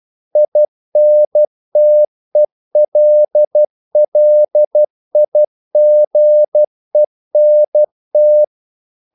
Morse code, heard as a keyed tone, about 12 wpm, 600 Hz. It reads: INTELLIGENT
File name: CW / Practice / EnglishQuotations0112